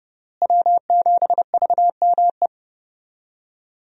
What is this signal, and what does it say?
Morse code, heard as a keyed tone, about 30 wpm, 700 Hz. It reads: W7VME
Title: USA Call Signs - 5 Characters - Repeated 3x - 30wpm